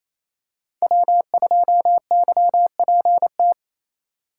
Morse code, heard as a keyed tone, about 28 wpm, 700 Hz. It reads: W2YPT
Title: USA Call Signs - 5 Characters - 8x Character Spacing - 28wpm